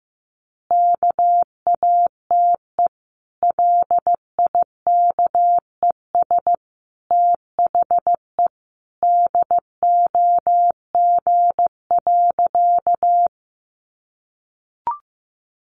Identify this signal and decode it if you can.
Morse code, heard as a keyed tone, about 15 wpm, 700 Hz. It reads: KATE LIKES THE DOG.
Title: Easy Phrases 15wpm